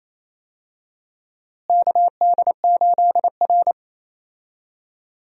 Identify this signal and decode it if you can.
Morse code, heard as a keyed tone, about 28 wpm, 700 Hz. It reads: KD8R
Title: USA Call Signs - 4 Characters - Repeated 3x - 28wpm